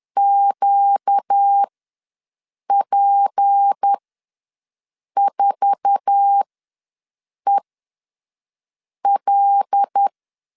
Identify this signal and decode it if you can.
Morse code, heard as a keyed tone, about 10 wpm, spaced as for 5 wpm, 800 Hz. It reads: QP4EL
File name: LEZIONI / 8 YJ45 / 8 YJ4 ws 10wpm 4020